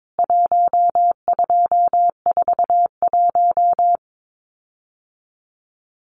Morse code, heard as a keyed tone, about 22 wpm, 700 Hz. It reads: 1241
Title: CWT Exchange 22wpm